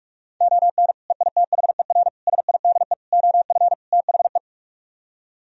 Morse code, heard as a keyed tone, about 45 wpm, 700 Hz. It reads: ON EITHER SIDE OF THE